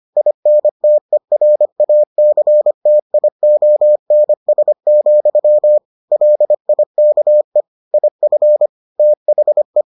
Morse code, heard as a keyed tone, about 25 wpm, 600 Hz. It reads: INTERACTIONS, LIKE IF THE